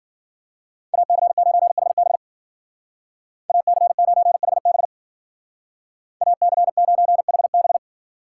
Morse code, heard as a keyed tone, about 50 wpm, 700 Hz. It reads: AC9HB AC9HB AC9HB
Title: Speed Racing - USA 2x2 Call Signs 40wpm